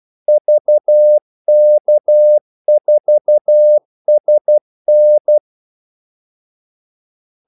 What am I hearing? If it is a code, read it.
Morse code, heard as a keyed tone, about 12 wpm, 600 Hz. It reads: VK4SN
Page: 2 ekstra ordmellemrum mellem kaldesignalerne